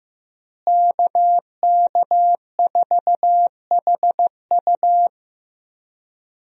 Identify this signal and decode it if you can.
Morse code, heard as a keyed tone, about 15 wpm, 700 Hz. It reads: KK4HU
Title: USA Call Signs - 5 Characters - Repeated 3x - 15wpm